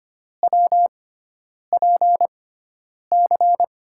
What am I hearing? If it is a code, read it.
Morse code, heard as a keyed tone, about 25 wpm, spaced as for 9 wpm, 700 Hz. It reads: WPC
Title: USA Call Signs - 6 Characters - 6x Character Spacing - 25wpm